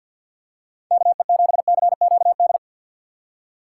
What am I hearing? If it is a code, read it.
Morse code, heard as a keyed tone, about 52 wpm, 700 Hz. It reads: KE7CQD
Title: USA Call Signs - 6 Characters - 4x Character Spacing - 50wpm